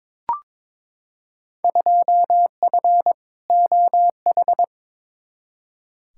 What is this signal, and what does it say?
Morse code, heard as a keyed tone, about 22 wpm, 700 Hz. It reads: 2F OH
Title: Field Day Exchanges - 22wpm